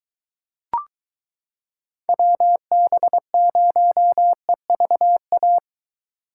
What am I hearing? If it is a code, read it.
Morse code, heard as a keyed tone, about 23 wpm, 700 Hz. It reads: WB0EVA